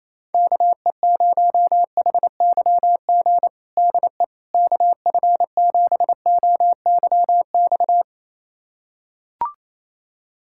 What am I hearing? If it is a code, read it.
Morse code, heard as a keyed tone, about 28 wpm, 700 Hz. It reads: KE0HYG DE KF7OYX
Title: USA Callsign DE Callsign - 6 Characters - 28wpm